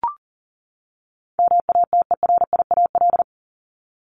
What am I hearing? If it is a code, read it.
Morse code, heard as a keyed tone, about 40 wpm, 700 Hz. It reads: MATERIAL
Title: ES:S Suffix - Top 100 Words with Base Words 40wpm